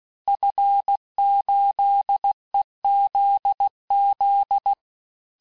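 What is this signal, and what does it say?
Morse code, heard as a keyed tone, about 16 wpm, 800 Hz. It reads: F8EZZ
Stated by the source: Morse course text, end of QSO, sent at 15 words per minute.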